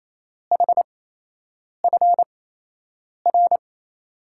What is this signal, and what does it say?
Morse code, heard as a keyed tone, about 28 wpm, spaced as for 8 wpm, 700 Hz. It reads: HFR
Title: USA Call Signs - 6 Characters - 8x Character Spacing - 28wpm